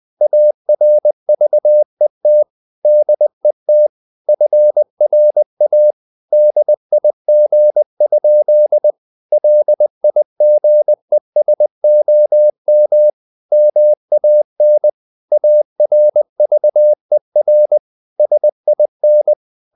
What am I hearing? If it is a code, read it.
Morse code, heard as a keyed tone, about 20 wpm, 600 Hz. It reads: ARVET DET FRA DIG? LIGESOM MAN ARVER SIN
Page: Ferie 20 wpm